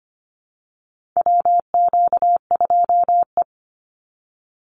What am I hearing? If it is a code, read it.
Morse code, heard as a keyed tone, about 25 wpm, 700 Hz. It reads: WQ2E
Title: Call Signs Repeated 2x - 25wpm